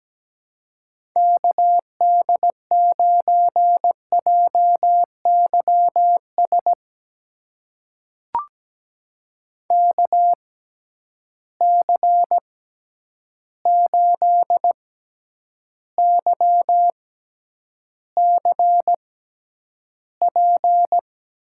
Morse code, heard as a keyed tone, about 17 wpm, spaced as for 9 wpm, 700 Hz. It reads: KD9JYS KC8YCP
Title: USA Call Signs - 6 Characters - 6x Character Spacing - 17wpm